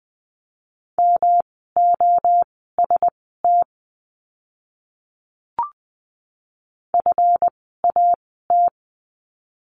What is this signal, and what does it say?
Morse code, heard as a keyed tone, about 20 wpm, spaced as for 15 wpm, 700 Hz. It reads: MOST FAT
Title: 139 Lesson 23f - Letter f - Words 20wpm